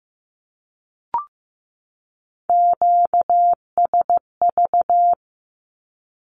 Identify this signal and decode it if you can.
Morse code, heard as a keyed tone, about 15 wpm, 700 Hz. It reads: QSV
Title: Q-Codes 15wpm